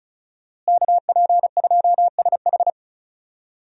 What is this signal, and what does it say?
Morse code, heard as a keyed tone, about 34 wpm, 700 Hz. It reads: KP2SH
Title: DXCC-100 Call Signs - 5 Characters - Repeated 2x - 35wpm